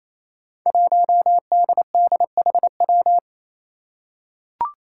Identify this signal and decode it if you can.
Morse code, heard as a keyed tone, about 29 wpm, 700 Hz. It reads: 1DDHW
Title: Five Challenging Letter-Number Combinations 28wpm